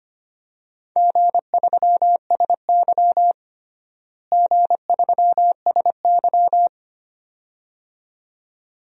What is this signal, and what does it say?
Morse code, heard as a keyed tone, about 25 wpm, 700 Hz. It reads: G3SY G3SY
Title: DXCC-100 Call Signs - 4 Characters - Repeated 2x - 25wpm